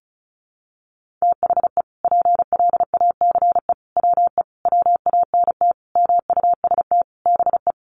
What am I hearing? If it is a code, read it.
Morse code, heard as a keyed tone, about 34 wpm, 700 Hz. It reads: THE PLACE WE WANT MUST BE
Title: Sentences from Top 1400 Words 35wpm